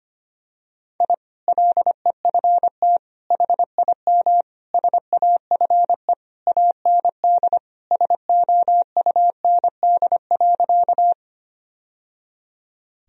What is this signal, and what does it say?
Morse code, heard as a keyed tone, about 25 wpm, 700 Hz. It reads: I LEFT HIM SAFE AND SOUND.
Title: Sentences from Top 1000 Words - Review Part 1 - 25wpm